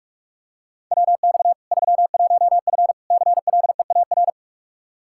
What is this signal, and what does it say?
Morse code, heard as a keyed tone, about 46 wpm, 700 Hz. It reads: WX 31F CLEAR
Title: Practice QSO - Exchange 2 - Info - 45wpm